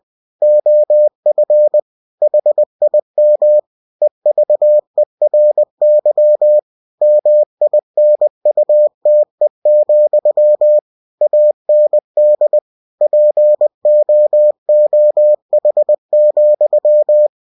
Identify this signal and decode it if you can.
Morse code, heard as a keyed tone, about 20 wpm, 600 Hz. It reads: OF HIM EVERY MINUTE, AND POOH,
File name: cw-books / winnie-the-pooh / winnie-20wpm-page-0071